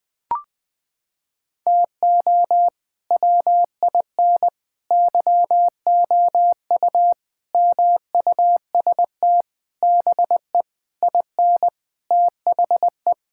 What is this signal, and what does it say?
Morse code, heard as a keyed tone, about 20 wpm, 700 Hz. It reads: TO WIN YOU MUST BE IN THE
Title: Sentences with 9-Word Phrases - 20wpm